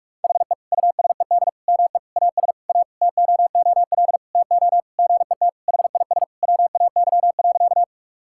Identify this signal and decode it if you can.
Morse code, heard as a keyed tone, about 41 wpm, 700 Hz. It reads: HE USED ME AS A TOOL TO GET HIS WAY.
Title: Sentences from Top 1000 Words - Review Part 6 - 45wpm